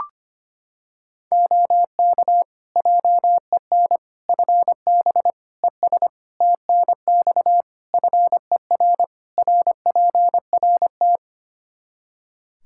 Morse code, heard as a keyed tone, about 25 wpm, 700 Hz. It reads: OK JEN FB ES TNX FER RPRT
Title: Practice QSO - Exchange 2 - Info - 25wpm